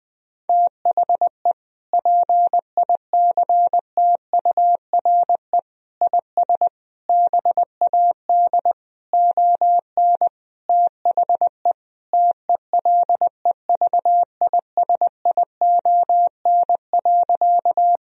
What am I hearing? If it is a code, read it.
Morse code, heard as a keyed tone, about 20 wpm, 700 Hz. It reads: THE PICTURE IS BAD ON THE TELEVISION.
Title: Sentences from Top 1000 Words - Review Part 4 - 20wpm